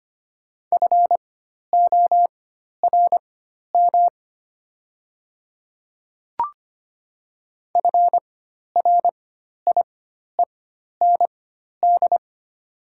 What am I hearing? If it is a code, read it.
Morse code, heard as a keyed tone, about 25 wpm, spaced as for 12 wpm, 700 Hz. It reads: FORM FRIEND
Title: 4x Character Spacing - Top 300 Words - 25wpm